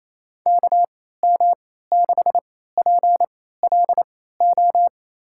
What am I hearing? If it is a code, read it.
Morse code, heard as a keyed tone, about 28 wpm, spaced as for 16 wpm, 700 Hz. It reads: KM6PLO
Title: USA Call Signs - 6 Characters - 3x Character Spacing - 28wpm